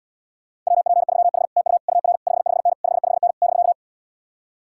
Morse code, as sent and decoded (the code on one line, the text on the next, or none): .-- .--. ...-- .... / - ..- / ..- .-. / ..... ..... -. / ..... ..... -. / -...-.-
WP3H TU UR 55N 55N <BK>